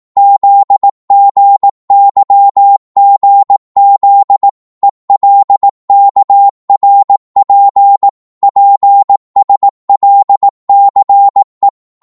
Morse code, heard as a keyed tone, about 18 wpm, 800 Hz. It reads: ZGYGZ ELKRP PSLCE
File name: Morse / ltr-18wpm